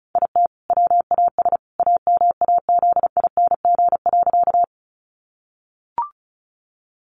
Morse code, heard as a keyed tone, about 34 wpm, 700 Hz. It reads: IT WAS AMAZING.
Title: Sentences from Top 1500 Words 35wpm